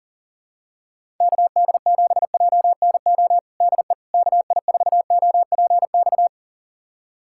Morse code, heard as a keyed tone, about 37 wpm, 700 Hz. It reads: KD7JNO DE KI4OPX
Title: USA Callsign DE Callsign 40wpm